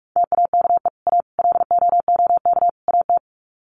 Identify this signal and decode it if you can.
Morse code, heard as a keyed tone, about 44 wpm, 700 Hz. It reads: TAKE A LOOK AT
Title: Sets of 4 Words 45wpm